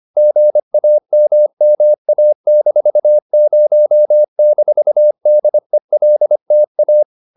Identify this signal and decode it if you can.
Morse code, heard as a keyed tone, about 25 wpm, 600 Hz. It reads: GAMMA-0-DELTA